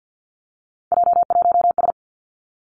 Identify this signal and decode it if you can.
Morse code, heard as a keyed tone, about 50 wpm, 700 Hz. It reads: J1S
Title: Three Challenging Letter-Number Combinations 50wpm